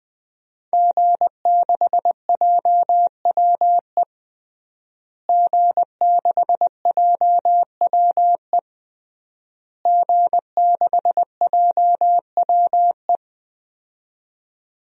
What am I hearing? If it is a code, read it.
Morse code, heard as a keyed tone, about 20 wpm, 700 Hz. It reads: G6JWE G6JWE G6JWE
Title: DXCC-100 Call Signs - 5 Characters - Repeated 3x - 20wpm